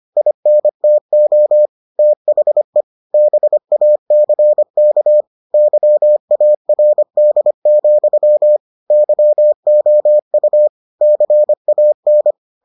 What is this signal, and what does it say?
Morse code, heard as a keyed tone, about 25 wpm, 600 Hz. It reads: INTO THE BACK YARD, YOU CAN